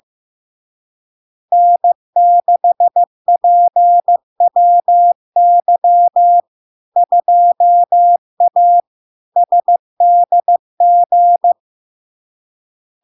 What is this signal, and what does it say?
Morse code, heard as a keyed tone, about 15 wpm, 700 Hz. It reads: N6PWY 2A SDG
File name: LARK-FD-2022-Log-in-CW-15wpm-700-Hz